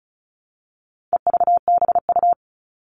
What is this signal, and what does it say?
Morse code, heard as a keyed tone, about 34 wpm, 700 Hz. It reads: EVBU
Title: All Cut Numbers - 4-Digit 35wpm